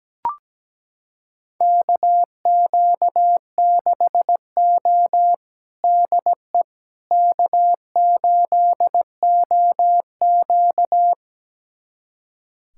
Morse code, heard as a keyed tone, about 17 wpm, 700 Hz. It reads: KQ6O DE K8OQ
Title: USA Callsign DE Callsign - 4 Characters - 17wpm